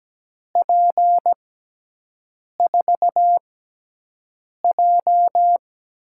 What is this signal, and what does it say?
Morse code, heard as a keyed tone, about 17 wpm, spaced as for 6 wpm, 700 Hz. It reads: P4J
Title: USA Call Signs - 6 Characters - 6x Character Spacing - 17wpm